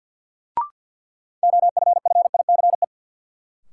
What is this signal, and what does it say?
Morse code, heard as a keyed tone, about 52 wpm, 700 Hz. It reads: OFFICE
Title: Sentences from Top 1000 Words - New Word Drill - 50wpm